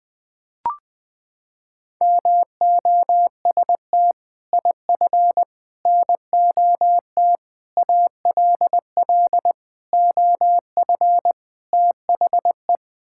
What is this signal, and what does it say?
Morse code, heard as a keyed tone, about 20 wpm, 700 Hz. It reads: MOST IF NOT ALL OF THE
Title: Sets of 6 Words - 20wpm